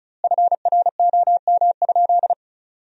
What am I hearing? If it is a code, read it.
Morse code, heard as a keyed tone, about 36 wpm, 700 Hz. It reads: FROM?
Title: 2x Word-Spacing - Sentences from Top 200 Words - 35wpm